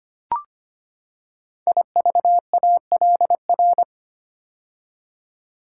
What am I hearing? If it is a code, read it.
Morse code, heard as a keyed tone, about 25 wpm, 700 Hz. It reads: IVALR
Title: Code Groups - 5 Letters - 25wpm